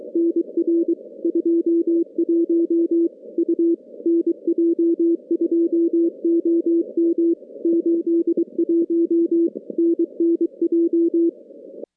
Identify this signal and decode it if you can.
Morse code, heard as a keyed tone, about 23 wpm, 380 Hz. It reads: NR 21 U NJ2OM 81 NNJ
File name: contests / CW-SS-k5tr- -2200